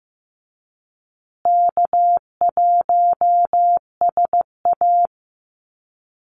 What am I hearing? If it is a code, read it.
Morse code, heard as a keyed tone, about 15 wpm, 700 Hz. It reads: K1SA
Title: Easy Call Signs Repeated 2x - 15wpm